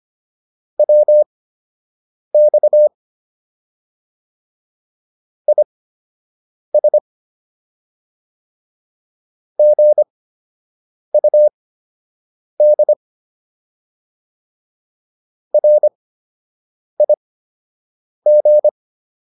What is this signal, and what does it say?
Morse code, heard as a keyed tone, about 25 wpm, spaced as for 7 wpm, 600 Hz. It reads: WX IS GUD RIG